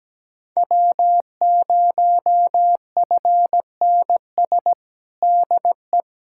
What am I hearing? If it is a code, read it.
Morse code, heard as a keyed tone, about 17 wpm, 700 Hz. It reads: W0FNS DE
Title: USA Callsign DE Callsign - 5 Characters - 17wpm